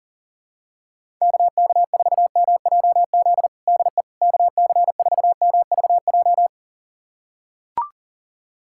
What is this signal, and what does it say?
Morse code, heard as a keyed tone, about 40 wpm, 700 Hz. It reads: KK4MJZ DE KC4MVJ
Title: USA Callsign DE Callsign - 6 Characters - 40wpm